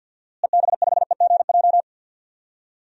Morse code, heard as a keyed tone, about 46 wpm, 700 Hz. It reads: EB5EGJ
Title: DXCC-100 Call Signs - 6 Characters - Repeated 2x - 50wpm